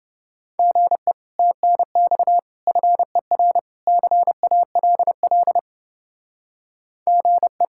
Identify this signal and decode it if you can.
Morse code, heard as a keyed tone, about 29 wpm, 700 Hz. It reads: GE TNX FER CALL GE